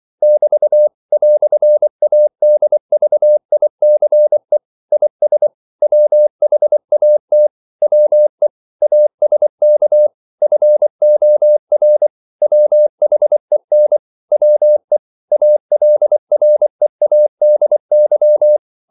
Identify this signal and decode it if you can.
Morse code, heard as a keyed tone, about 24 wpm, 600 Hz. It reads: <BT> "ADVICE IS WHAT WE ASK FOR WHEN WE ALREADY
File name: CW / Practice / EnglishQuotations0224